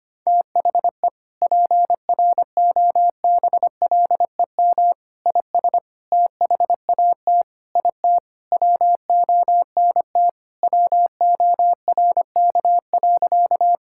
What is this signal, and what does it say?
Morse code, heard as a keyed tone, about 25 wpm, 700 Hz. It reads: THE PROBLEM IS THAT IT WONT WORK.
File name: Sentences with 3-Word Phrases - 25wpm